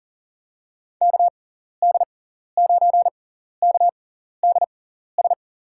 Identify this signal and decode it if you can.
Morse code, heard as a keyed tone, about 40 wpm, spaced as for 14 wpm, 700 Hz. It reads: KD9KDS
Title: USA Call Signs - 6 Characters - 6x Character Spacing - 40wpm